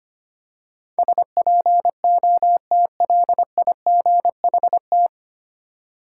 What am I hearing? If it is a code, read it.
Morse code, heard as a keyed tone, about 25 wpm, 700 Hz. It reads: SPOTLIGHT
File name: practice / 25wpm / 9 Letter Words - First 3 Letters Given - 25wpm